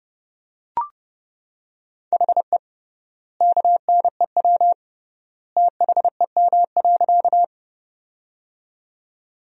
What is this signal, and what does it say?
Morse code, heard as a keyed tone, about 32 wpm, 700 Hz. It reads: HE KNEW THEM.
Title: 3x Word-Spacing - Sentences from Top 400 Words - 30wpm